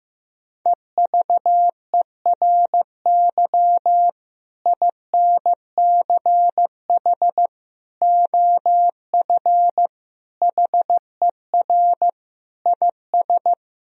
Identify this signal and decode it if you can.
Morse code, heard as a keyed tone, about 15 wpm, 700 Hz. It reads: EVERY INCH OF HER IS
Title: Sentences from Top 1000 Words - Review Part 3 - 15wpm